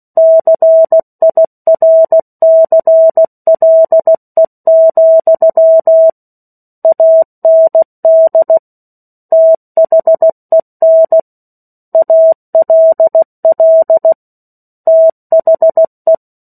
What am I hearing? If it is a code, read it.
Morse code, heard as a keyed tone, about 16 wpm, 650 Hz. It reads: CIRCLE, AND THEN ALL THE
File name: 03-Chapter-03 16 WPM